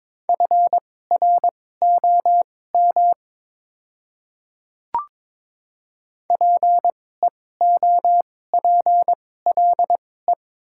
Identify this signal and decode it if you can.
Morse code, heard as a keyed tone, about 22 wpm, spaced as for 16 wpm, 700 Hz. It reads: FROM PEOPLE
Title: 2x Character Spacing - Top 100 Words - 22wpm